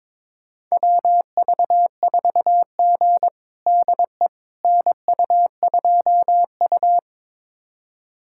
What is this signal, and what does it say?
Morse code, heard as a keyed tone, about 22 wpm, 700 Hz. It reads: WV4G DE NU2U
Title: USA Callsign DE Callsign - 4 Characters - 22wpm